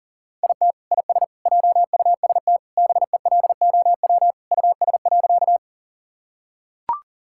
Morse code, heard as a keyed tone, about 40 wpm, 700 Hz. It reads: IT IS JUST BELOW US.
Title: Sentences from Top 1000 Words - Review Part 5 - 40wpm